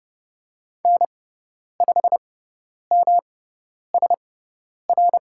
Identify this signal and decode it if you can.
Morse code, heard as a keyed tone, about 30 wpm, spaced as for 10 wpm, 700 Hz. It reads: N5MSR
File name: Lesson 18g - Letter m - Call Signs 30wpm